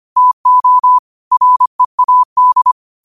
Morse code, heard as a keyed tone, about 25 wpm, 1000 Hz. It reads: TO READ
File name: qotdfiles / 25wpm